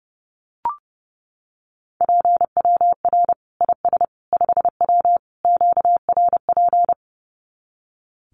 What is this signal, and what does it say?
Morse code, heard as a keyed tone, about 30 wpm, 700 Hz. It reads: PWR IS 5W QRP
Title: Common QSO Phrases 30wpm